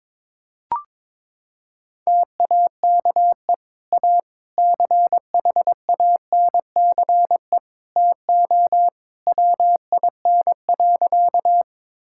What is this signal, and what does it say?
Morse code, heard as a keyed tone, about 22 wpm, 700 Hz. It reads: TAKE A CHANCE TO WIN.
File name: practice / 22wpm / Sentences with 3-Word Phrases - 22wpm